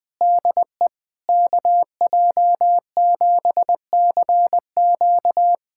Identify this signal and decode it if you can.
Morse code, heard as a keyed tone, about 20 wpm, 700 Hz. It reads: DE KJ7CQ
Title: USA Callsign DE Callsign - 5 Characters - 20wpm